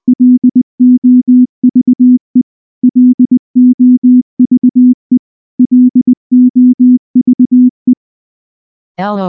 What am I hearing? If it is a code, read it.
Morse code, heard as a keyed tone, about 20 wpm, 250 Hz. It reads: LOVE LOVE LOVE
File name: morse / morse-training-20WPM